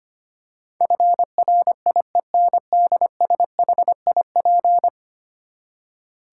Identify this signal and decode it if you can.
Morse code, heard as a keyed tone, about 25 wpm, 700 Hz. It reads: FRIENDSHIP
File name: Ten Letter Words - 25wpm